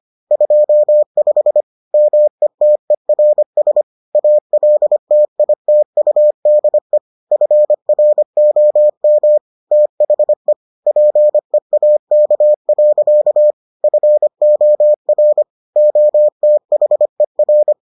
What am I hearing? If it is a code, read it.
Morse code, heard as a keyed tone, about 25 wpm, 600 Hz. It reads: 25 METERS ALTITUDE FROM THE PEAK. FOR OTHER